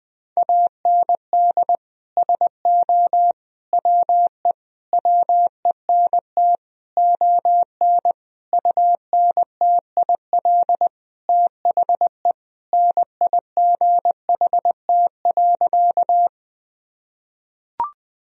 Morse code, heard as a keyed tone, about 20 wpm, 700 Hz. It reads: AND SO WE WENT ON UNTIL THE NIGHT.
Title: Sentences from Top 1000 Words - Review Part 1 - 20wpm